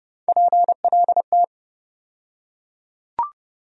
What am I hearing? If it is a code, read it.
Morse code, heard as a keyed tone, about 30 wpm, 700 Hz. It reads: PLT
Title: USA Call Sign Postfixes - 3 Letters 30wpm